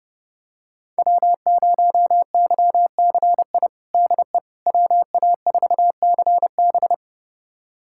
Morse code, heard as a keyed tone, about 30 wpm, 700 Hz. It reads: W0YCI DE WA4CB
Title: USA Callsign DE Callsign - 5 Characters - 30wpm